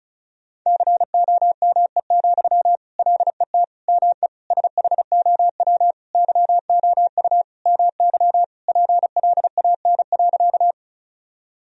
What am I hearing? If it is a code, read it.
Morse code, heard as a keyed tone, about 34 wpm, 700 Hz. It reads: COME, LET ME SHOW YOU MY PLAN.
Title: Sentences from Top 1000 Words - Review Part 2 - 35wpm